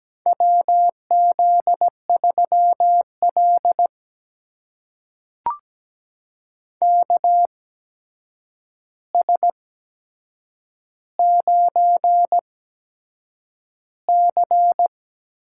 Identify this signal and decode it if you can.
Morse code, heard as a keyed tone, about 17 wpm, spaced as for 7 wpm, 700 Hz. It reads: WZ3L KS9C
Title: USA Call Signs - 4 Characters - 8x Character Spacing - 17wpm